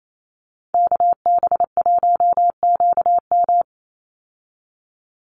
Morse code, heard as a keyed tone, about 28 wpm, 700 Hz. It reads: KB1QM
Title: USA Call Signs 28wpm